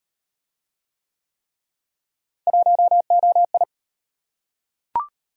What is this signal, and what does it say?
Morse code, heard as a keyed tone, about 37 wpm, 700 Hz. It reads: 1OI